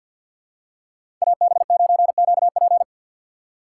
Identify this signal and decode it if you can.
Morse code, heard as a keyed tone, about 52 wpm, 700 Hz. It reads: AB9QP
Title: USA Call Signs - 5 Characters - 6x Character Spacing - 50wpm